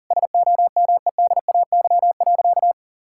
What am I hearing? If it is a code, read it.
Morse code, heard as a keyed tone, about 40 wpm, 700 Hz. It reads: SOMEDAY.
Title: Sentences from Top 1600 Words 40wpm